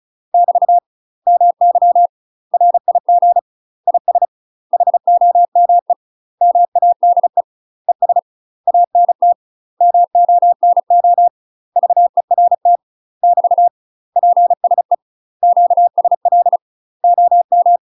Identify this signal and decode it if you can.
Morse code, heard as a keyed tone, about 33 wpm, 700 Hz. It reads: <BT> MY RIG IS HOME MADE ES ANT MONO VERT <BT> PSE QSL OM
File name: UK MASTER QSOs TRAINING / 35WPM / UK MASTER QSOs 0012